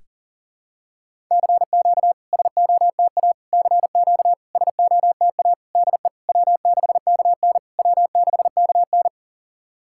Morse code, heard as a keyed tone, about 37 wpm, 690 Hz. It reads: CQ SOTA CQ SOTA DE W6KN W6KN
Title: SOTA QSO Practice 40wpm